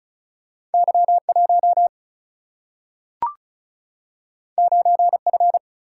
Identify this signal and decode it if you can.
Morse code, heard as a keyed tone, about 36 wpm, 700 Hz. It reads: Y 1 9 F
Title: Single-Letter Single-Number 35wpm